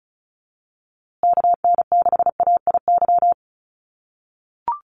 Morse code, heard as a keyed tone, about 34 wpm, 700 Hz. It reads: KN6AIY
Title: USA Call Signs Repeated 2x 35wpm